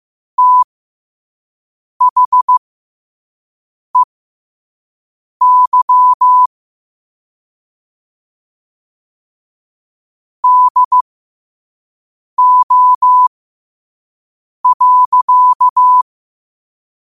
Morse code, heard as a keyed tone, about 15 wpm, spaced as for 5 wpm, 1000 Hz. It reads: THEY DO.